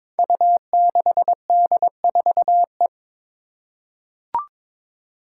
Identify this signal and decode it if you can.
Morse code, heard as a keyed tone, about 22 wpm, 700 Hz. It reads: U6D4E
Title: Five Challenging Letter-Number Combinations 22wpm